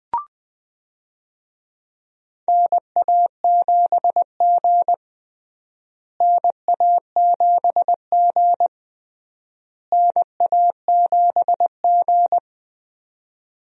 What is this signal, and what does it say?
Morse code, heard as a keyed tone, about 20 wpm, 700 Hz. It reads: NA7G NA7G NA7G
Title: USA Call Signs - 4 Characters - Repeated 3x - 20wpm